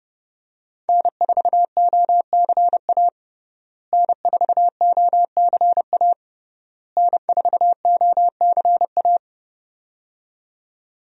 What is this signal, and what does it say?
Morse code, heard as a keyed tone, about 30 wpm, 700 Hz. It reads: N4OCA N4OCA N4OCA
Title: USA Call Signs - 5 Characters - Repeated 3x - 30wpm